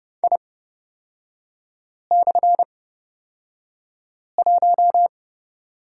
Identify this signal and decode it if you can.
Morse code, heard as a keyed tone, about 30 wpm, spaced as for 5 wpm, 700 Hz. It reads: I/1
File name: Lesson 36e - BT - Three Characters 30wpm